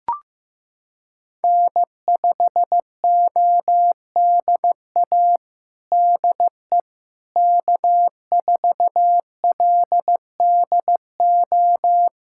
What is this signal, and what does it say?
Morse code, heard as a keyed tone, about 16 wpm, 700 Hz. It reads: N5ODA DE K4LDO
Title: USA Callsign DE Callsign - 5 Characters - 15wpm